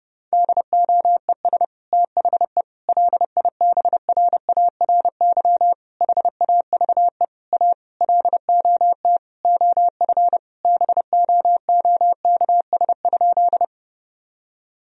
Morse code, heard as a keyed tone, about 32 wpm, 700 Hz. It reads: DOES THE LIBRARY HAVE A LOT OF BOOKS?
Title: Sentences from Top 1000 Words - Review Part 4 - 30wpm